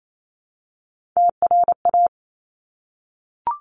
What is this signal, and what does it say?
Morse code, heard as a keyed tone, about 28 wpm, 700 Hz. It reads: TRA
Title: Top Three Letter Combinations in Words 28wpm